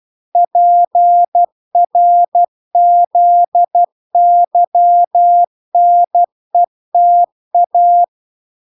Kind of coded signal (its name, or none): Morse code